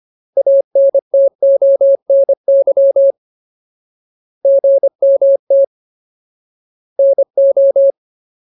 Morse code, heard as a keyed tone, about 25 wpm, 550 Hz. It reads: ANTONY GMT NO
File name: Ultimate Word List / 25WPM Ultimate